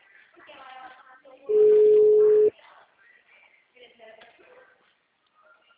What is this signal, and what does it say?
Ringback tone (Europe, 425 Hz), with speech.